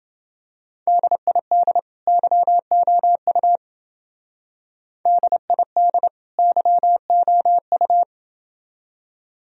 Morse code, heard as a keyed tone, about 28 wpm, 700 Hz. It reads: DID YOU DID YOU